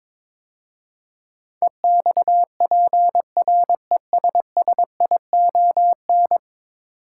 Morse code, heard as a keyed tone, about 22 wpm, 700 Hz. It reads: EXPRESSION
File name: practice / 22wpm / 10 Letter Words - First 3 Letters Given - 22wpm